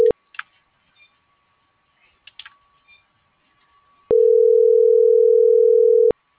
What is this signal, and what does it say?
Ringback tone (North America), 440 + 480 Hz.